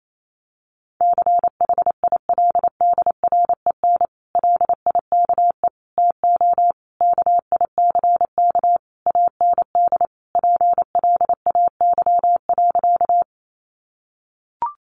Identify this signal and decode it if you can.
Morse code, heard as a keyed tone, about 28 wpm, 700 Hz. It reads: CHILDREN LIKE TO KICK AND PLAY.
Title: Sentences from Top 1300 Words 28wpm